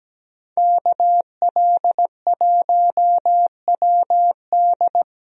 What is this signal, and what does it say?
Morse code, heard as a keyed tone, about 17 wpm, 700 Hz. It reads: KL1WD
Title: USA Call Signs - 5 Characters - Repeated 2x - 17wpm